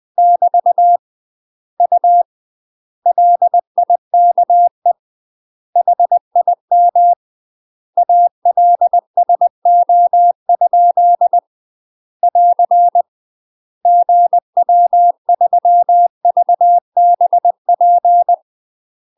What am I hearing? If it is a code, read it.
Morse code, heard as a keyed tone, about 20 wpm, 700 Hz. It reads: <BT> U LIKE HIM ALSO? <AR> GW3VBP
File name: UK MASTER QSOs TRAINING / 20WPM / UK MASTER QSOs 0060